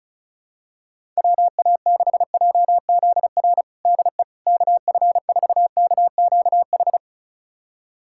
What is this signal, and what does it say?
Morse code, heard as a keyed tone, about 36 wpm, 700 Hz. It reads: WA6JZR DE KF4KQH
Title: USA Callsign DE Callsign - 6 Characters - 35wpm